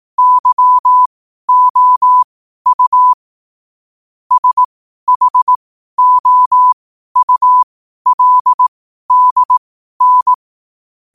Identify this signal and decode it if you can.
Morse code, heard as a keyed tone, about 18 wpm, spaced as for 12 wpm, 1000 Hz. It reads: YOU SHOULDN
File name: qotdfiles / 12wpm